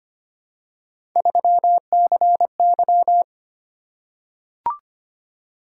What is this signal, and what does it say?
Morse code, heard as a keyed tone, about 25 wpm, 700 Hz. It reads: 3CY
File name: Three Letter-Number Combinations - 25wpm